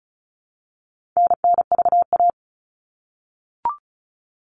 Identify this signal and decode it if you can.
Morse code, heard as a keyed tone, about 34 wpm, 700 Hz. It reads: NNVA
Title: All Cut Numbers - 4-Digit 35wpm